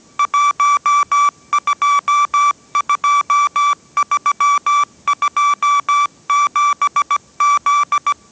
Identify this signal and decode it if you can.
Morse code, heard as a keyed tone, about 17 wpm, 1160 Hz. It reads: 122327Z
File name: 18wpm